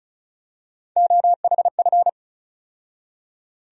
Morse code, heard as a keyed tone, about 34 wpm, 700 Hz. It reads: OHF
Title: USA Call Sign Postfixes - 3 Letters 35wpm